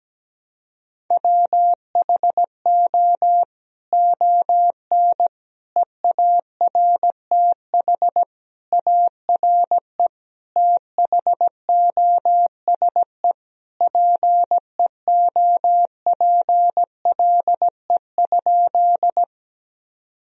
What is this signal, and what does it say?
Morse code, heard as a keyed tone, about 17 wpm, 700 Hz. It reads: WHO ON EARTH ARE THOSE PEOPLE?
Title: Sentences from Top 1000 Words - Review Part 1 - 17wpm